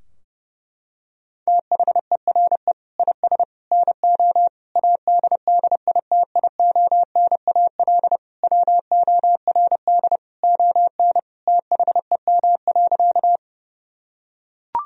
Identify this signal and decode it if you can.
Morse code, heard as a keyed tone, about 30 wpm, 700 Hz. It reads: THERE IS NO ADDITIONAL WORD ON THEM.
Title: Sentences from Top 1000 Words - Review Part 4 - 30wpm